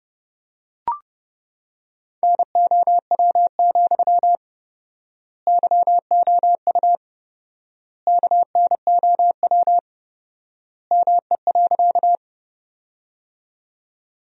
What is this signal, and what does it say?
Morse code, heard as a keyed tone, about 30 wpm, 700 Hz. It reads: NOW, YOU KNOW ME.
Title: 4x Word-Spacing - Sentences from Top 200 Words - 30wpm